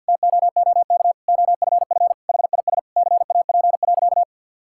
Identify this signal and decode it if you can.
Morse code, heard as a keyed tone, about 50 wpm, 700 Hz. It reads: TOOK OFF HIS CAP.
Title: Sentences from Top 1600 Words 50wpm